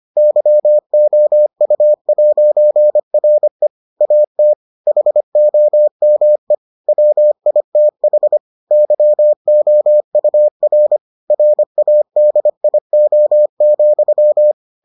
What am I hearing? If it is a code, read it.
Morse code, heard as a keyed tone, about 25 wpm, 600 Hz. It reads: YOU'RE AT HOME WITH YOUR RADIO,